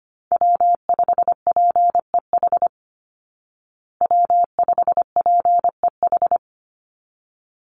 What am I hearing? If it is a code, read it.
Morse code, heard as a keyed tone, about 25 wpm, 700 Hz. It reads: W5PEH W5PEH
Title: Call Signs Repeated 3x - 25wpm